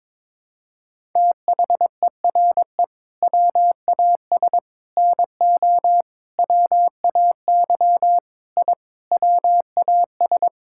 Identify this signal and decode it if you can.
Morse code, heard as a keyed tone, about 22 wpm, 700 Hz. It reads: THERE WAS NO WAY I WAS
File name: practice / 22wpm / Sentences with 8-Word Phrases - 22wpm